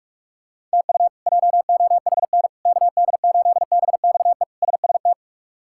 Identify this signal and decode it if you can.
Morse code, heard as a keyed tone, about 46 wpm, 700 Hz. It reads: TU JOHN KD8BXE SST
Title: K1USN SST QSO Practice - 45wpm